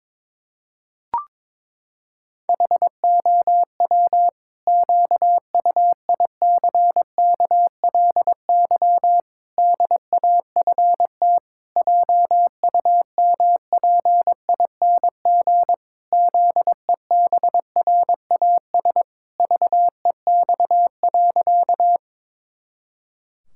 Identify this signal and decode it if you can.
Morse code, heard as a keyed tone, about 22 wpm, 700 Hz. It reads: HOW QUICKLY DAFT JUMPING ZEBRAS VEX.
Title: Pangram Challenge v2 - 22wpm